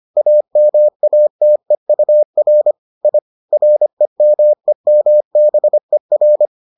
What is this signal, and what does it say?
Morse code, heard as a keyed tone, about 25 wpm, 600 Hz. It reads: AMATEUR I REMEMBER